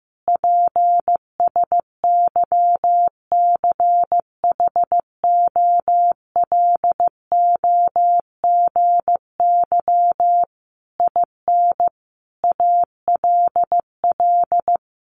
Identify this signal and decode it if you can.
Morse code, heard as a keyed tone, about 15 wpm, 700 Hz. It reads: PSYCHOLOGY IN ALL
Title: Sentences from Top 1400 Words 15wpm